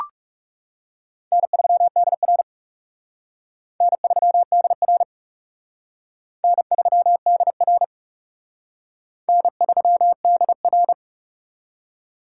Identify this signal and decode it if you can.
Morse code, heard as a keyed tone, about 37 wpm, 700 Hz. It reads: N3DR N3DR N3DR N3DR
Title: Speed Racing - USA 1x2 Call Signs 30wpm